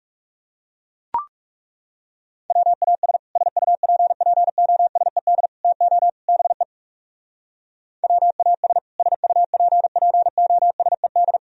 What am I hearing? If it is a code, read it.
Morse code, heard as a keyed tone, about 40 wpm, 700 Hz. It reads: WAS SUPPOSED TO BE WAS SUPPOSED